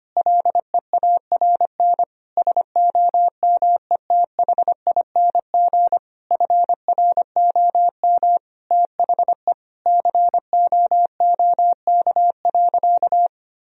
Morse code, heard as a keyed tone, about 25 wpm, 700 Hz. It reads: LEARN SOMETHING FROM THE COOK.
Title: Sentences from Top 1000 Words - Review Part 4 - 25wpm